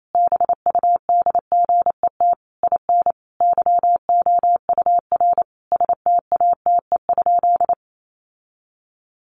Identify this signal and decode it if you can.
Morse code, heard as a keyed tone, about 28 wpm, 700 Hz. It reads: BUDGET IN YOUR STATE?
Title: Sentences from Top 1200 Words 28wpm